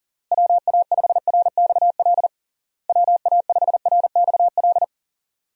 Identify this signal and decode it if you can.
Morse code, heard as a keyed tone, about 40 wpm, 700 Hz. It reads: WA5RXL WA5RXL
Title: USA Call Signs - 6 Characters - Repeated 3x - 40wpm